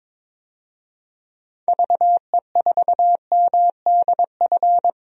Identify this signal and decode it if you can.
Morse code, heard as a keyed tone, about 21 wpm, 700 Hz. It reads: VE4MDF
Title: DXCC-100 Call Signs - 6 Characters - 22wpm